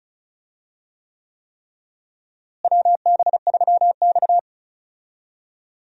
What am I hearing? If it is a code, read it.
Morse code, heard as a keyed tone, about 36 wpm, 700 Hz. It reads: WB3X
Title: USA Call Signs - 4 Characters - Repeated 3x - 35wpm